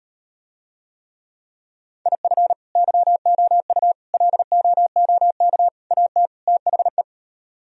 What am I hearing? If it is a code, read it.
Morse code, heard as a keyed tone, about 38 wpm, 700 Hz. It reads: IF YOU LOOK AT THE